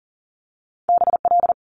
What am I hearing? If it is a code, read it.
Morse code, heard as a keyed tone, about 41 wpm, 700 Hz. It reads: BL
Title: Multi-letter Phonemes Quick Fire 40wpm